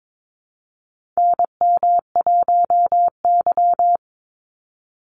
Morse code, heard as a keyed tone, about 22 wpm, 700 Hz. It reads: NM1Y
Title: Call Signs - 22wpm